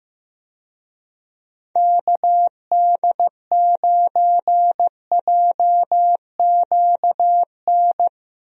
Morse code, heard as a keyed tone, about 16 wpm, 700 Hz. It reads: KD9JQN
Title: USA Call Signs - 6 Characters - 15wpm